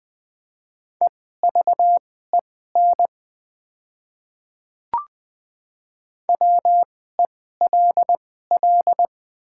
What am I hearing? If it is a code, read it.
Morse code, heard as a keyed tone, about 20 wpm, spaced as for 14 wpm, 700 Hz. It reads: EVEN WELL
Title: 2x Character Spacing - Top 200 Words - 20wpm